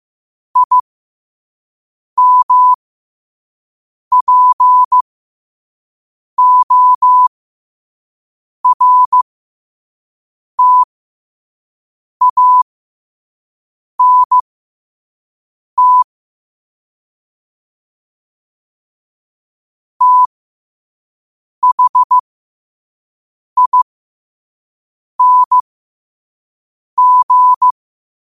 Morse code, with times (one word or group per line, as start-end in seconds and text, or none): 0.6-16.0 IMPORTANT
20.0-27.7 THING